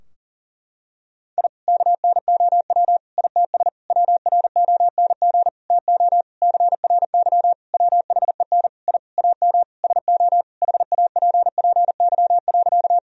Morse code, heard as a keyed tone, about 37 wpm, 700 Hz. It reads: I KNOW ITS WRONG TO CRY WHEN I AM SO HAPPY.
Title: Sentences from Top 1000 Words - Review Part 2 - 40wpm